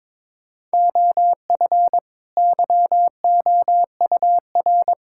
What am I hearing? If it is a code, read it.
Morse code, heard as a keyed tone, about 22 wpm, 700 Hz. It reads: OF YOUR
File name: Build A Phrase - 2-Word Phrases - 22wpm